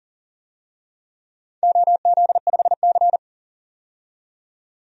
Morse code, heard as a keyed tone, about 42 wpm, 700 Hz. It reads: OZ5C
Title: DXCC-100 Call Signs - 4 Characters - 40wpm